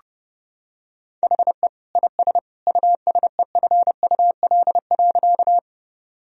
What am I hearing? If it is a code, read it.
Morse code, heard as a keyed tone, about 30 wpm, 700 Hz. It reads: HE IS USEFUL.
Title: Sentences from Top 1000 Words - Review Part 4 - 30wpm